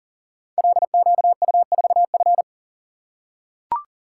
Morse code, heard as a keyed tone, about 40 wpm, 700 Hz. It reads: LQU4F
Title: Code Groups - 5 Letter:Numbers - 40wpm